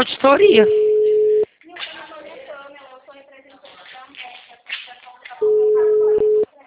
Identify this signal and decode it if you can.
Ringback tone (Europe, 425 Hz), with speech.
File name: out- -2007- - - .57490